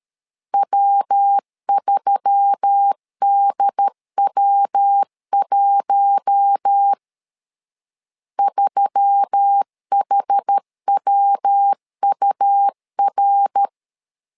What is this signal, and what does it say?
Morse code, heard as a keyed tone, about 13 wpm, 800 Hz. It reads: W3DW1 3HWUR
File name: LEZIONI / 4 Q1S3 12wpm 2023